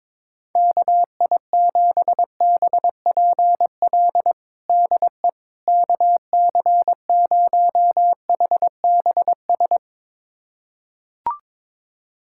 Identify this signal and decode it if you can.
Morse code, heard as a keyed tone, about 21 wpm, 700 Hz. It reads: KI7BPL DE KC0HBS
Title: USA Callsign DE Callsign - 6 Characters - 22wpm